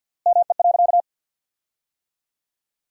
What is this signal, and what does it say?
Morse code, heard as a keyed tone, about 53 wpm, 700 Hz. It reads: ME.
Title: 4x Word-Spacing - Sentences from Top 200 Words - 50wpm